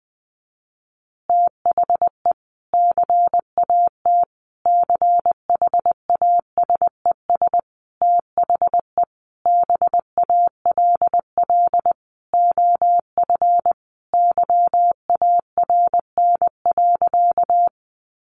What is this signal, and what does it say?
Morse code, heard as a keyed tone, about 20 wpm, 700 Hz. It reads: THE CAT CHASES THE BALL OF YARN.
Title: Additional Easy Phrases 20wpm